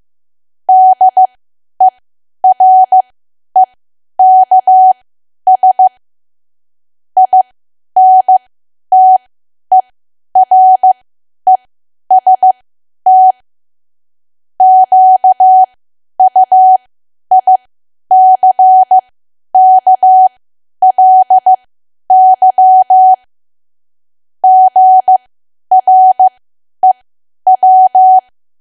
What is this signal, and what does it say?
Morse code, heard as a keyed tone, about 15 wpm, spaced as for 10 wpm, 750 Hz. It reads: DEREKS INTEREST QUICKLY GREW
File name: cwa / W1AW-Level2 / 10WPM